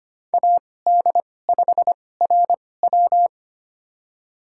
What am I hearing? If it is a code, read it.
Morse code, heard as a keyed tone, about 25 wpm, spaced as for 18 wpm, 700 Hz. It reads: AD5RW
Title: USA Call Signs - 5 Characters - 2x Character Spacing - 25wpm